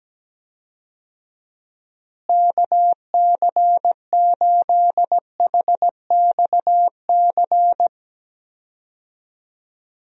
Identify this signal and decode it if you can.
Morse code, heard as a keyed tone, about 17 wpm, 700 Hz. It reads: KC8HXC
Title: USA Call Signs - 6 Characters - Repeated 2x - 17wpm